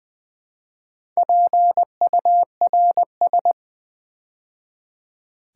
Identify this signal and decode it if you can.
Morse code, heard as a keyed tone, about 20 wpm, 700 Hz. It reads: PURS
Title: Build A Word - 2 Letters At A Time - 6-Letter Words - 20wpm